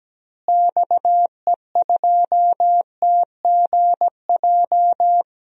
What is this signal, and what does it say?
Morse code, heard as a keyed tone, about 17 wpm, 700 Hz. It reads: XE2TGJ
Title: DXCC-100 Call Signs - 6 Characters - Repeated 2x - 17wpm